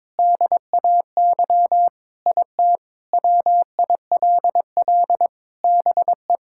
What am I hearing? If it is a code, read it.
Morse code, heard as a keyed tone, about 22 wpm, 700 Hz. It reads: DAY IT WILL BE
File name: practice / 22wpm / Sentences with 4-Word Phrases - 22wpm